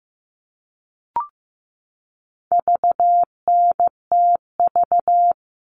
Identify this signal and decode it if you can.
Morse code, heard as a keyed tone, about 15 wpm, 700 Hz. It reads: VNTV
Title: All Cut Numbers - 4-Digit 15wpm